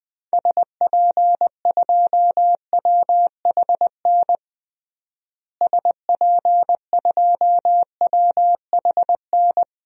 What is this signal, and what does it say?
Morse code, heard as a keyed tone, about 20 wpm, 700 Hz. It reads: SP2WHN SP2WHN
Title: DXCC-100 Call Signs - 6 Characters - Repeated 2x - 20wpm